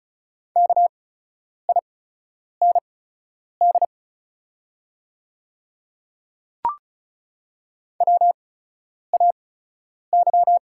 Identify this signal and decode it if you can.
Morse code, heard as a keyed tone, about 36 wpm, spaced as for 10 wpm, 700 Hz. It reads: KIND WAY
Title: 8x Character Spacing - Top 200 Words - 35wpm